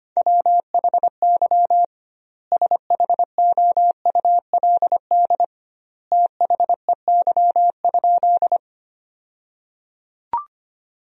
Morse code, heard as a keyed tone, about 25 wpm, 700 Hz. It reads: WHY SHOULD THEY?
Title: 2x Word-Spacing - Sentences from Top 300 Words - 25wpm